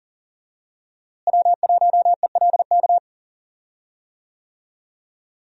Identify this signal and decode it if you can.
Morse code, heard as a keyed tone, about 40 wpm, 700 Hz. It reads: W1ELK
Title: USA Call Signs - 5 Characters - Repeated 3x - 40wpm